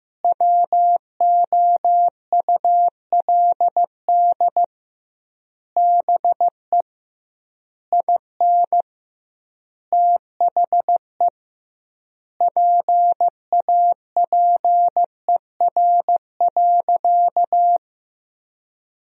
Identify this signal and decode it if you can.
Morse code, heard as a keyed tone, about 15 wpm, 700 Hz. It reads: WOULD BE IN THE PAPER.
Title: 2x Word-Spacing - Sentences from Top 300 Words - 15wpm